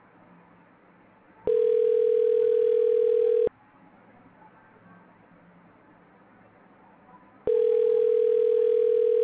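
Ringback tone (North America), 440 + 480 Hz.
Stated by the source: Outbound telephone call reaching an Avaya IP Office phone system.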